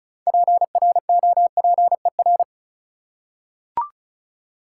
Morse code, heard as a keyed tone, about 36 wpm, 700 Hz. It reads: PROPER
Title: Six Letter Words - 35wpm